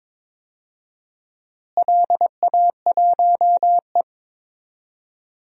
Morse code, heard as a keyed tone, about 22 wpm, 700 Hz. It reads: LA1E
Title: DXCC-100 Call Signs - 4 Characters - Repeated 3x - 22wpm